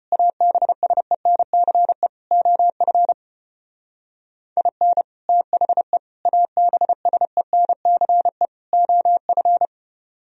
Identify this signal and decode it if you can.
Morse code, heard as a keyed tone, about 33 wpm, 700 Hz. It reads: ABSENCE OF IN THE ABSENCE OF